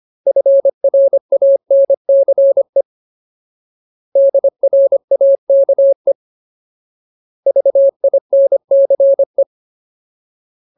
Morse code, heard as a keyed tone, about 25 wpm, 550 Hz. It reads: FRANCE DRAKE VINCE
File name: Ultimate Word List / 25WPM Ultimate